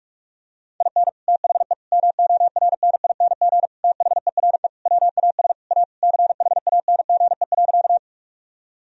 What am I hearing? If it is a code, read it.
Morse code, heard as a keyed tone, about 43 wpm, 700 Hz. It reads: IN THE MORNING THERE WAS A CHANGE.
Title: Sentences from Top 1000 Words - Review Part 1 - 45wpm